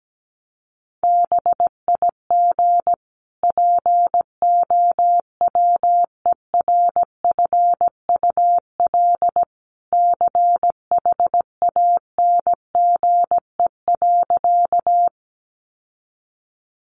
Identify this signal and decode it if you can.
Morse code, heard as a keyed tone, about 17 wpm, 700 Hz. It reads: BIG POWERFUL CHANGE.
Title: Sentences from Top 1300 Words 17wpm